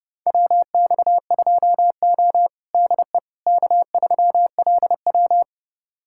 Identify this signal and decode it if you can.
Morse code, heard as a keyed tone, about 32 wpm, 700 Hz. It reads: WX2O DE K3LW
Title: USA Callsign DE Callsign - 4 Characters - 30wpm